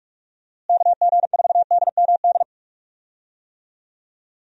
Morse code, heard as a keyed tone, about 46 wpm, 700 Hz. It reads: KG4DMD
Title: USA Call Signs - 6 Characters - Repeated 3x - 45wpm